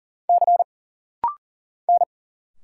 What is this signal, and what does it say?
Morse code, heard as a keyed tone, about 40 wpm, 700 Hz. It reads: C N
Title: Single Letters - Rapid-Fire 40wpm